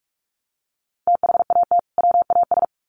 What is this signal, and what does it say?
Morse code, heard as a keyed tone, about 44 wpm, 700 Hz. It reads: THAT WAS
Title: Sets of 2 Words 45wpm